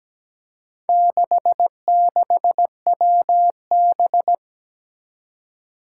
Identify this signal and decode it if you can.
Morse code, heard as a keyed tone, about 17 wpm, 700 Hz. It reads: 66WB
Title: Four Challenging Letter-Number Combinations 17wpm